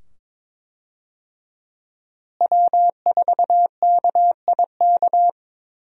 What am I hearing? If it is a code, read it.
Morse code, heard as a keyed tone, about 21 wpm, 700 Hz. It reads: W4KIK
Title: USA Call Signs - 5 Characters - Repeated 2x - 22wpm